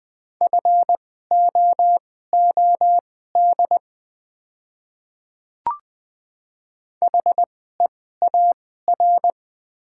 Morse code, heard as a keyed tone, about 20 wpm, spaced as for 14 wpm, 700 Hz. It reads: FOOD HEAR
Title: 2x Character Spacing - Top 500 Words - 20wpm